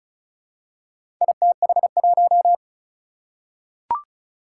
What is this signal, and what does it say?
Morse code, heard as a keyed tone, about 36 wpm, 700 Hz. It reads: ITH1
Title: Four Challenging Letter-Number Combinations 35wpm